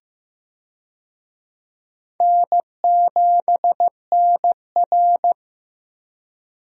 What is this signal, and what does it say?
Morse code, heard as a keyed tone, about 15 wpm, 700 Hz. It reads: N7NR
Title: USA Call Signs - 4 Characters - Repeated 2x - 15wpm